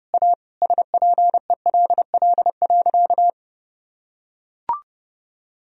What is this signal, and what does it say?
Morse code, heard as a keyed tone, about 30 wpm, 700 Hz. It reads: A SPELL.
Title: Sentences from Top 1000 Words - Review Part 1 - 30wpm